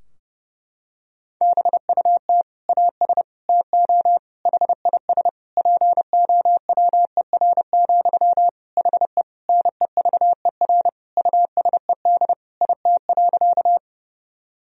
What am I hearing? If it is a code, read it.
Morse code, heard as a keyed tone, about 30 wpm, 700 Hz. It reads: BUT AS TO HIS POWER, HE NEVER USED IT.
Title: Sentences from Top 1000 Words - Review Part 2 - 30wpm